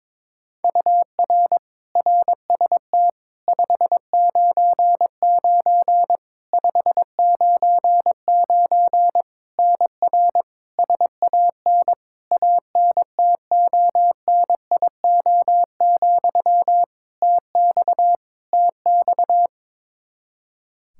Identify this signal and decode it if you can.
Morse code, heard as a keyed tone, about 22 wpm, 700 Hz. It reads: UR RST 599 599 NR SAN ANTONIO, TX TX
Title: K1USN SST QSO Practice 22wpm